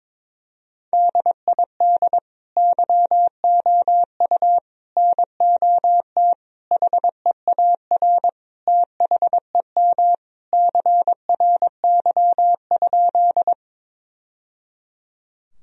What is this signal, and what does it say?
Morse code, heard as a keyed tone, about 22 wpm, 700 Hz. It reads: DID YOU NOT HEAR THEM CRY?
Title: Sentences from Top 1000 Words - Review Part 2 - 22wpm